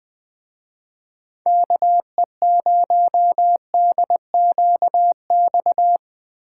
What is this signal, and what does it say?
Morse code, heard as a keyed tone, about 20 wpm, 700 Hz. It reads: KE0DQX